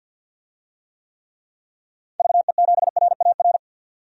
Morse code, heard as a keyed tone, about 50 wpm, 700 Hz. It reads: VE7RAR
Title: DXCC-100 Call Signs - 6 Characters - 50wpm